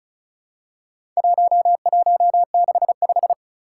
Morse code, heard as a keyed tone, about 36 wpm, 700 Hz. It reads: 1165
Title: Four Challenging Letter-Number Combinations 35wpm